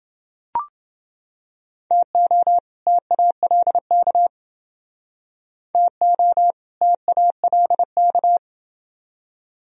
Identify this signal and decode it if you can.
Morse code, heard as a keyed tone, about 29 wpm, 700 Hz. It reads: TO TALK TO TALK